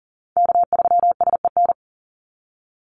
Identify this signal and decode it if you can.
Morse code, heard as a keyed tone, about 40 wpm, 700 Hz. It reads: K3SEN
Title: Call Signs Repeated 3x - 40wpm